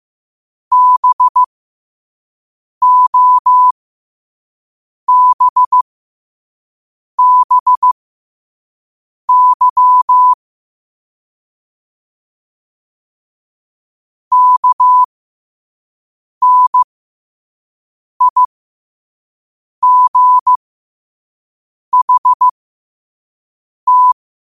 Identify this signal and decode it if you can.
Morse code, heard as a keyed tone, about 15 wpm, spaced as for 5 wpm, 1000 Hz. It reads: BOBBY KNIGHT
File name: qotdfiles / 5wpm